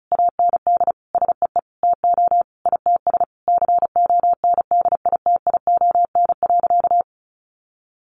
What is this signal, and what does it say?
Morse code, heard as a keyed tone, about 34 wpm, 700 Hz. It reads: AND SEE TO ITS CONDITION.
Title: Sentences from Top 1100 Words 35wpm